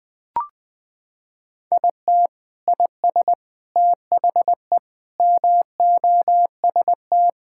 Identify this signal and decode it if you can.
Morse code, heard as a keyed tone, about 20 wpm, 700 Hz. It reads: IT IS THE MOST
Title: Sentences from Top 1000 Words - Review Part 6 - 20wpm